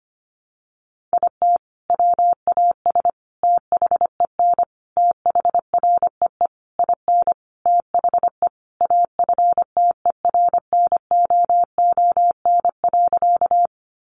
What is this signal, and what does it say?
Morse code, heard as a keyed tone, about 25 wpm, 700 Hz. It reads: IT WAS THEN THREE IN THE AFTERNOON.
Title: Sentences from Top 1500 Words 25wpm